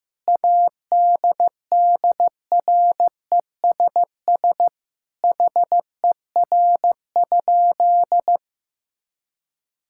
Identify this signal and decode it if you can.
Morse code, heard as a keyed tone, about 15 wpm, 700 Hz. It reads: ADDRESS HER?
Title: Sentences from Top 1000 Words - Review Part 6 - 15wpm